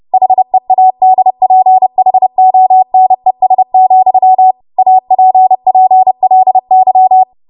Morse code, heard as a keyed tone, about 30 wpm, 750 Hz. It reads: HEADPHONES, APPLY
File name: Archive / 30 WPM / 30WPM